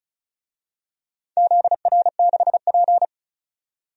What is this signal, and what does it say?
Morse code, heard as a keyed tone, about 34 wpm, 700 Hz. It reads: ZR6P
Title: DXCC-100 Call Signs - 4 Characters - 35wpm